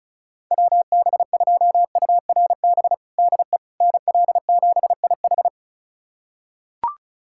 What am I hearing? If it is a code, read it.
Morse code, heard as a keyed tone, about 36 wpm, 700 Hz. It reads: WB2URB DE NL7IH
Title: USA Callsign DE Callsign 35wpm